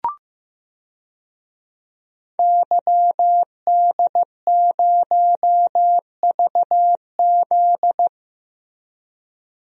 Morse code, heard as a keyed tone, about 15 wpm, 700 Hz. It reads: YD0VZ
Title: DXCC-100 Call Signs - 5 Characters - Repeated 2x - 15wpm